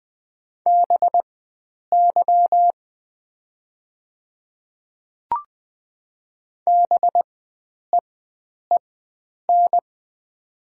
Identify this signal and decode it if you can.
Morse code, heard as a keyed tone, about 20 wpm, spaced as for 9 wpm, 700 Hz. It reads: BY BEEN
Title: 4x Character Spacing - Top 300 Words - 20wpm